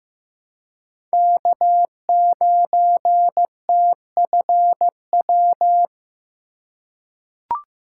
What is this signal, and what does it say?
Morse code, heard as a keyed tone, about 16 wpm, 700 Hz. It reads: K9TFW
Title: USA Call Signs - 5 Characters - 4x Character Spacing - 15wpm